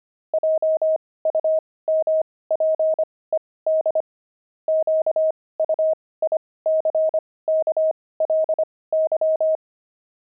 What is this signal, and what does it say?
Morse code, heard as a keyed tone, about 25 wpm, spaced as for 18 wpm, 630 Hz. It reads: JUMPED QUICKLY
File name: pangram-fs-18-cs-25-jumping